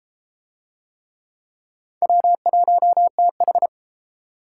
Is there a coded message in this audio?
Yes: Morse code from 2.0 to 3.7 s, heard as a keyed tone, about 30 wpm, 700 Hz. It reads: W1TH